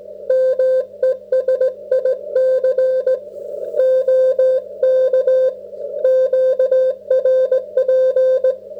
Morse code, heard as a keyed tone, about 17 wpm, 530 Hz. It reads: MESIC OK QRP